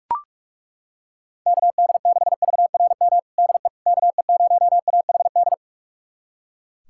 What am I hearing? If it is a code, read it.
Morse code, heard as a keyed tone, about 41 wpm, 700 Hz. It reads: KD6VRM DE KE0AHD
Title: USA Callsign DE Callsign 45wpm